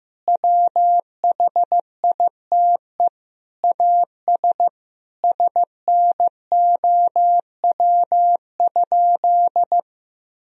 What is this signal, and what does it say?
Morse code, heard as a keyed tone, about 15 wpm, 700 Hz. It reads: WHITE AS SNOW?
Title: Sentences from Top 1000 Words - Review Part 3 - 15wpm